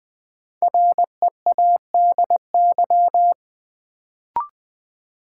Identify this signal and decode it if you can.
Morse code, heard as a keyed tone, about 20 wpm, 700 Hz. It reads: READY
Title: Sentences from Top 400 Words - New Word Drill - 20wpm